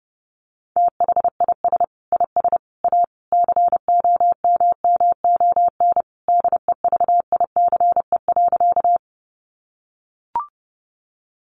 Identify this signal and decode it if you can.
Morse code, heard as a keyed tone, about 30 wpm, 700 Hz. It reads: THIS IS A COMMON DEVICE.
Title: Sentences from Top 1500 Words 30wpm